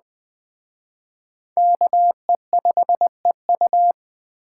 Morse code, heard as a keyed tone, about 21 wpm, 700 Hz. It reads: KE5EU